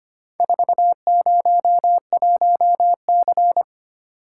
Morse code, heard as a keyed tone, about 25 wpm, 700 Hz. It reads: 401C
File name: practice / 25wpm / Four Letter-Number Combinations - 25wpm